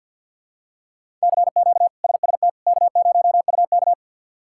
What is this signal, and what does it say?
Morse code, heard as a keyed tone, about 52 wpm, 690 Hz. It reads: CQ SST K0UK
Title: K1USN SST QSO Practice - 50wpm